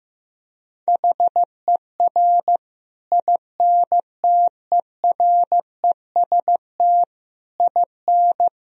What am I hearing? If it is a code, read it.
Morse code, heard as a keyed tone, about 15 wpm, 700 Hz. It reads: HER INTEREST IN
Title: Sentences from Top 1000 Words - Review Part 2 - 15wpm